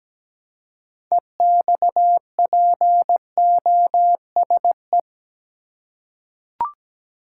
Morse code, heard as a keyed tone, about 17 wpm, 700 Hz. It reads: EXPOSE
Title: Six Letter Words - 17wpm